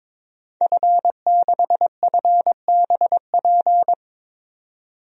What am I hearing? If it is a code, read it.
Morse code, heard as a keyed tone, about 22 wpm, 700 Hz. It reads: F6FBP
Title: DXCC-100 Call Signs - 5 Characters - 22wpm